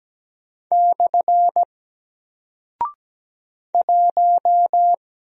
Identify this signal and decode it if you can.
Morse code, heard as a keyed tone, about 17 wpm, 700 Hz. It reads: / 1
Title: Single Character (Letter:Number:Punctuation:Prosign) Alt 17wpm